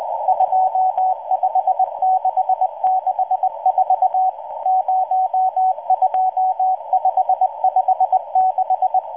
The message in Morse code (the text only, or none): 25664 02556